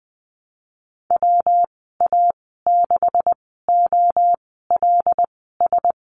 Morse code, heard as a keyed tone, about 20 wpm, spaced as for 15 wpm, 700 Hz. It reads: WA6OLS
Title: 161 Lesson 26g - Letter g - Call Signs 20wpm